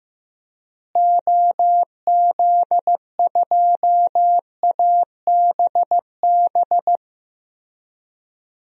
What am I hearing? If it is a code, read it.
Morse code, heard as a keyed tone, about 15 wpm, 700 Hz. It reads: OZ2ABB
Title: DXCC-100 Call Signs - 6 Characters - Repeated 3x - 15wpm